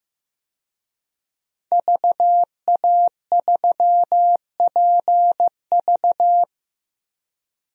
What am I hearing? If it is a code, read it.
Morse code, heard as a keyed tone, about 16 wpm, 700 Hz. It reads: VA3PV
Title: DXCC-100 Call Signs - 5 Characters - Repeated 3x - 15wpm